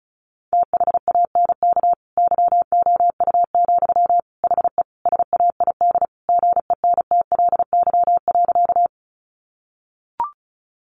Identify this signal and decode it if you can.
Morse code, heard as a keyed tone, about 34 wpm, 700 Hz. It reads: THANK YOU, HE SAID GENTLY.
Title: Sentences from Top 1500 Words 35wpm